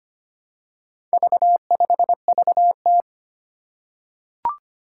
Morse code, heard as a keyed tone, about 25 wpm, 700 Hz. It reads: V5VT
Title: Four Challenging Letter-Number Combinations 25wpm